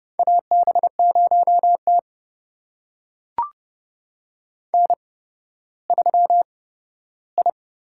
Morse code, heard as a keyed tone, about 29 wpm, spaced as for 14 wpm, 700 Hz. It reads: AB0T N3I
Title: USA Call Signs - 4 Characters - 8x Character Spacing - 30wpm